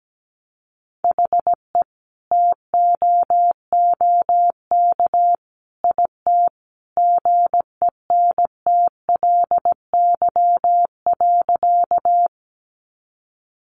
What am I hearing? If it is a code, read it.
Morse code, heard as a keyed tone, about 17 wpm, 700 Hz. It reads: HE TOOK IT GENTLY.
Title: Sentences from Top 1500 Words 17wpm